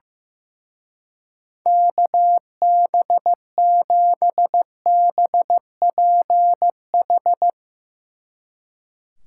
Morse code, heard as a keyed tone, about 15 wpm, 700 Hz. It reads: KB7BPH
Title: Speed Racing - USA 2x3 Call Signs 15wpm